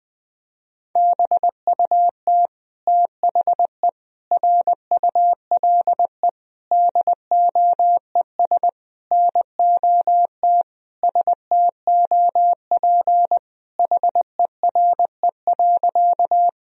Morse code, heard as a keyed tone, about 20 wpm, 700 Hz. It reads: BUT THE RULE DOES NOT STOP HERE.
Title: Sentences from Top 1000 Words - Review Part 2 - 20wpm